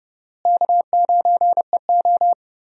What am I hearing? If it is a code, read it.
Morse code, heard as a keyed tone, about 32 wpm, 700 Hz. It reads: K9EO
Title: USA Call Signs - 4 Characters - Repeated 3x - 30wpm